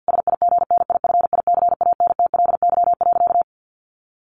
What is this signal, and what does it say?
Morse code, heard as a keyed tone, about 48 wpm, 700 Hz. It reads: SIGNIFICANTLY.
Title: Sentences from Top 1500 Words 50wpm